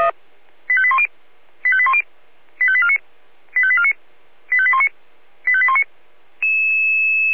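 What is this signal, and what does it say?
Five-tone selective call (ZVEI), heard as quick runs of five tones, tones of about 70 ms. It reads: 85810 85810 85849 85849 85819 85819
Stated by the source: ZVEI five-tone selective call 85875 received on an SDR.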